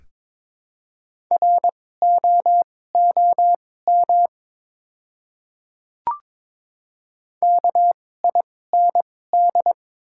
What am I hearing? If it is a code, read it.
Morse code, heard as a keyed tone, about 22 wpm, spaced as for 16 wpm, 700 Hz. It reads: ROOM KIND
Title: 2x Character Spacing - Top 300 Words - 22wpm